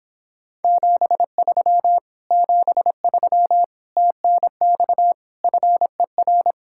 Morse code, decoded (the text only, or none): 73 73 TNX FER